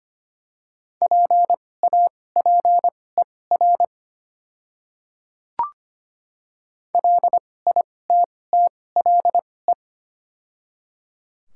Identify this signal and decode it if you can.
Morse code, heard as a keyed tone, about 25 wpm, spaced as for 18 wpm, 690 Hz. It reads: PAPER LITTLE
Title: 2x Character Spacing - Top 300 Words - 25wpm